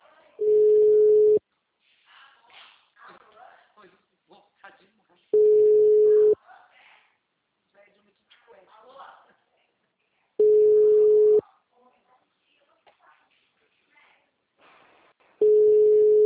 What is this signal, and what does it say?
Ringback tone (Europe, 425 Hz), with speech.